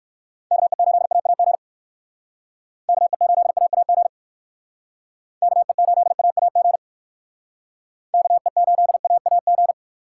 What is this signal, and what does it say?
Morse code, heard as a keyed tone, about 58 wpm, 700 Hz. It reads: KE8AAG KE8AAG KE8AAG KE8AAG